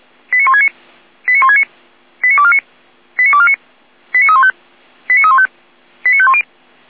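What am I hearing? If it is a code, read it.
Five-tone selective call (ZVEI), heard as quick runs of five tones, tones of about 70 ms. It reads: 89169 89169 89249 89249 89315 89315 89510
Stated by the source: ZVEI five-tone call received by SDR.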